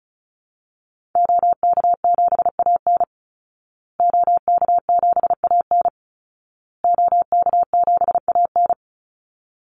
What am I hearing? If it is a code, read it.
Morse code, heard as a keyed tone, about 34 wpm, 700 Hz. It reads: OK7AN OK7AN OK7AN
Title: Call Signs Repeated 2x - 35wpm